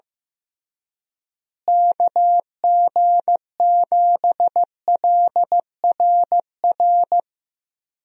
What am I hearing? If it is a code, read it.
Morse code, heard as a keyed tone, about 15 wpm, 700 Hz. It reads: KG7LRR
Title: USA Call Signs - 6 Characters - Repeated 3x - 15wpm